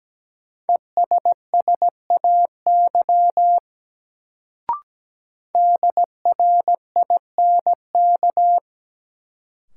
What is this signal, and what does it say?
Morse code, heard as a keyed tone, about 17 wpm, 700 Hz. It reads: ESSAY DRINK
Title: Sentences from Top 1500 Words - New Word Drill - 17wpm